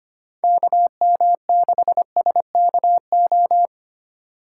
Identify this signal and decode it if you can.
Morse code, heard as a keyed tone, about 25 wpm, 700 Hz. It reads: KM6SKO
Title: USA Call Signs - 2x Character Spacing - 25wpm